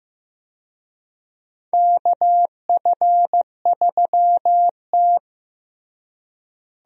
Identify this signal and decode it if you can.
Morse code, heard as a keyed tone, about 15 wpm, 700 Hz. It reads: KF3T
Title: USA Call Signs - 4 Characters - Repeated 3x - 15wpm